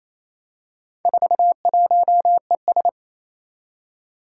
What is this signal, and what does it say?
Morse code, heard as a keyed tone, about 29 wpm, 700 Hz. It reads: 41ES
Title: Four Challenging Letter-Number Combinations 28wpm